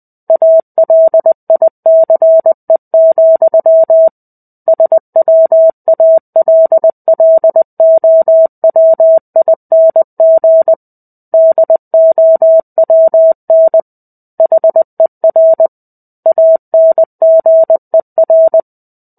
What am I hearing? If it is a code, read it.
Morse code, heard as a keyed tone, about 20 wpm, 650 Hz. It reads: ALICE, SWALLOWING DOWN HER ANGER
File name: 05-Chapter-05 20 WPM